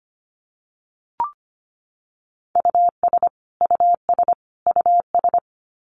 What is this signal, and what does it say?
Morse code, heard as a keyed tone, about 25 wpm, 700 Hz. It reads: US US US
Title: Top 200 Words Repeated 25wpm